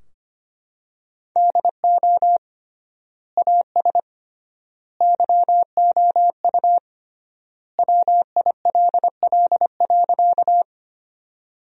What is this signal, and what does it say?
Morse code, heard as a keyed tone, about 25 wpm, 700 Hz. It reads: DO AS YOU WILL.
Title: 3x Word-Spacing - Sentences from Top 100 Words - 25wpm